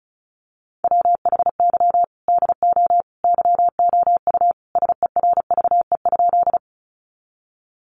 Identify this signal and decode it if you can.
Morse code, heard as a keyed tone, about 34 wpm, 700 Hz. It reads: WHY DO YOU SERVE?
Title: Sentences from Top 400 Words 35wpm